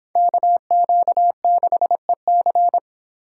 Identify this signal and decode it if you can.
Morse code, heard as a keyed tone, about 26 wpm, 700 Hz. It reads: KQ6EC